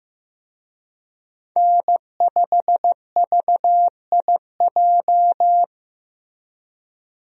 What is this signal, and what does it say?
Morse code, heard as a keyed tone, about 15 wpm, 700 Hz. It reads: N5VIJ
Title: USA Call Signs - 5 Characters - Repeated 2x - 15wpm